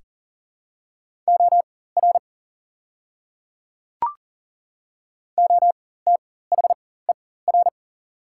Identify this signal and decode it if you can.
Morse code, heard as a keyed tone, about 40 wpm, spaced as for 19 wpm, 700 Hz. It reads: OR OTHER
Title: 4x Character Spacing - Top 100 Words - 40wpm